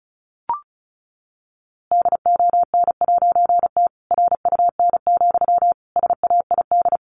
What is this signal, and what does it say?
Morse code, heard as a keyed tone, about 34 wpm, 700 Hz. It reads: DON'T RUN, SAID
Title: A Sound Of Thunder 35wpm